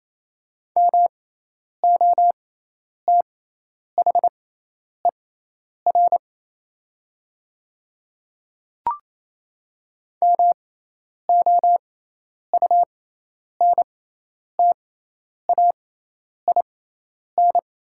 Morse code, heard as a keyed tone, about 28 wpm, spaced as for 10 wpm, 700 Hz. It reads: MOTHER MOUNTAIN
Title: 6x Character Spacing - Top 300 Words - 28wpm